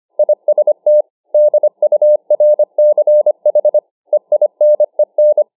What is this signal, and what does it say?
Morse code, heard as a keyed tone, about 25 wpm, 600 Hz. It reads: IST DURCH EINEN